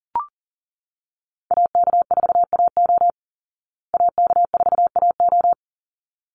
Morse code, heard as a keyed tone, about 40 wpm, 700 Hz. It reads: AK4AO AK4AO
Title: Call Signs Repeated 3x - 40wpm